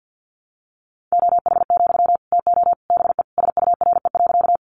Morse code, heard as a keyed tone, about 50 wpm, 700 Hz. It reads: OH, TO BE SURE.
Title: Sentences from Top 500 Words 50wpm